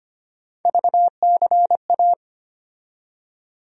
Morse code, heard as a keyed tone, about 25 wpm, 700 Hz. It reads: VCA
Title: USA Call Sign Postfixes - 3 Letters 25wpm